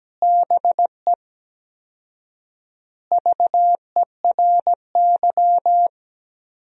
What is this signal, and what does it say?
Morse code, heard as a keyed tone, about 17 wpm, 700 Hz. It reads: BE VERY
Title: 4x Word-Spacing - Sentences from Top 200 Words - 17wpm